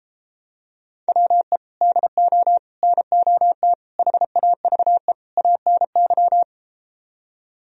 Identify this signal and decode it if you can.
Morse code, heard as a keyed tone, about 33 wpm, 700 Hz. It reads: WE DO NOT HAVE ANY